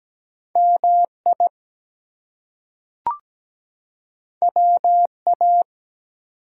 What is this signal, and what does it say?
Morse code, heard as a keyed tone, about 17 wpm, 700 Hz. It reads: MI WA
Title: US States and Canadian Provinces 17wpm